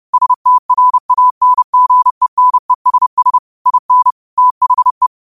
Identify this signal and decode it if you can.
Morse code, heard as a keyed tone, about 30 wpm, 1000 Hz. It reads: STRANGENESS IN THE